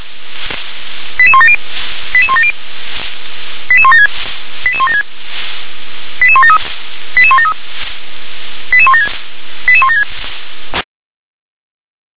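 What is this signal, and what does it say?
Five-tone selective call (ZVEI), heard as quick runs of five tones, tones of about 70 ms. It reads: 88170 88170 88176 88176 88173 88173 88176 88176
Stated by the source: ZVEI five-tone selective call received on SDR.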